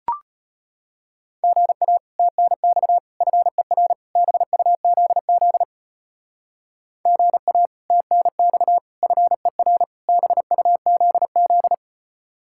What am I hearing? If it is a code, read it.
Morse code, heard as a keyed tone, about 30 wpm, 700 Hz. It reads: GA TNX FER BUZZ GA TNX FER BUZZ
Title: Speed-Racing Common QSO Phrases - 25wpm